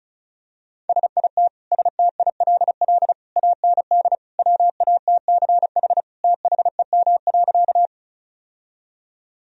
Morse code, heard as a keyed tone, about 34 wpm, 700 Hz. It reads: SIT STILL AND WATCH THEM.
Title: Sentences from Top 1000 Words - Review Part 1 - 35wpm